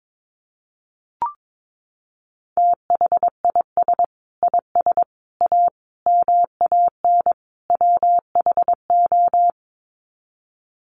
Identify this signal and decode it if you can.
Morse code, heard as a keyed tone, about 22 wpm, 700 Hz. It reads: THIS IS A MAN WHO
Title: Sets of 5 Words Encore2 22wpm